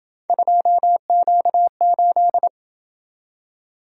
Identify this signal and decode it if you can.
Morse code, heard as a keyed tone, about 27 wpm, 700 Hz. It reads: 2Q8